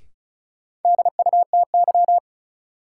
Morse code, heard as a keyed tone, about 36 wpm, 700 Hz. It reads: DUTY
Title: Sentences from Top 1500 Words - New Word Drill Alt - 35wpm